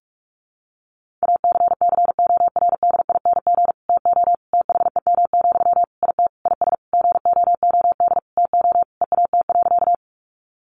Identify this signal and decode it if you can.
Morse code, heard as a keyed tone, about 44 wpm, 700 Hz. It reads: ACCORDING TO THEM, IT IS GOOD TO EAT.
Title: Sentences from Top 800 Words 45wpm